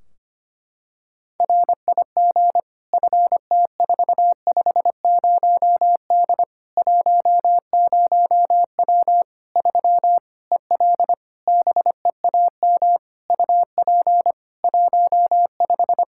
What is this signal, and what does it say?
Morse code, heard as a keyed tone, about 25 wpm, 700 Hz. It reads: RIG FT450D 10W 3 EL BEAM UP 15
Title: Practice QSO - Exchange 2 - Info - 25wpm